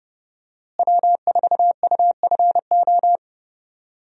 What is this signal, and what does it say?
Morse code, heard as a keyed tone, about 30 wpm, 700 Hz. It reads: W4UFO
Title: USA Call Signs - 5 Characters - Repeated 2x - 30wpm